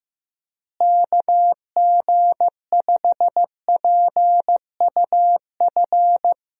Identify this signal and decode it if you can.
Morse code, heard as a keyed tone, about 15 wpm, 700 Hz. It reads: KG5PUF
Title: USA Call Signs - 6 Characters - Repeated 2x - 15wpm